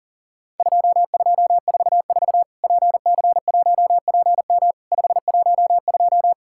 Morse code, heard as a keyed tone, about 40 wpm, 700 Hz. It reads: 2244 PC1PM 512
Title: Worked All Europe DX Contest - QTC Exchange - 40wpm